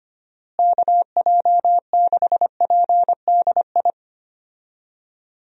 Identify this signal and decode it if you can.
Morse code, heard as a keyed tone, about 25 wpm, 700 Hz. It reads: KJ6PDI
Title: USA Call Signs - 6 Characters - Repeated 3x - 25wpm